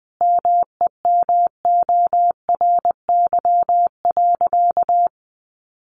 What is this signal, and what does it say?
Morse code, heard as a keyed tone, about 20 wpm, 700 Hz. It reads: MEMORY.
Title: Sentences from Top 1300 Words 20wpm